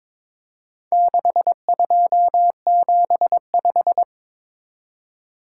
Morse code, heard as a keyed tone, about 23 wpm, 700 Hz. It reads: 6275
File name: practice / 22wpm / Four Letter-Number Combinations - 22wpm